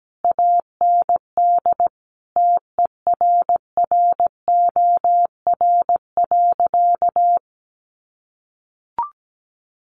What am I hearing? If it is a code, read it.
Morse code, heard as a keyed tone, about 17 wpm, 700 Hz. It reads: AND TERROR.
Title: A Sound Of Thunder 17wpm